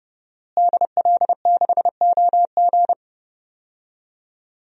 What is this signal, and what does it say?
Morse code, heard as a keyed tone, about 31 wpm, 700 Hz. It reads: DL6OG
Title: DXCC-100 Call Signs - 5 Characters - Repeated 2x - 30wpm